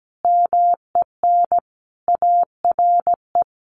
Morse code, heard as a keyed tone, about 17 wpm, 700 Hz. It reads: MEN ARE
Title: Sentences from Top 1300 Words 17wpm